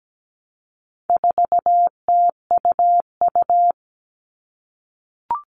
Morse code, heard as a keyed tone, about 17 wpm, 700 Hz. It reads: 4TUU
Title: All Cut Numbers - 4-Digit 17wpm